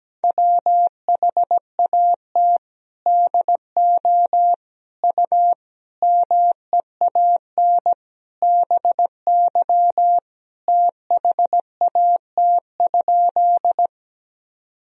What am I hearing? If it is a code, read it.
Morse code, heard as a keyed tone, about 17 wpm, 700 Hz. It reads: WHAT DO U MEAN BY THAT?
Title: Sentences from Top 200 Words and Abbreviations 17wpm